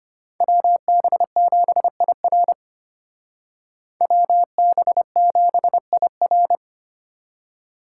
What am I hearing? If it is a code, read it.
Morse code, heard as a keyed tone, about 31 wpm, 700 Hz. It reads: WB7IR WB7IR
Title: Speed Racing - USA 2x2 Call Signs 25wpm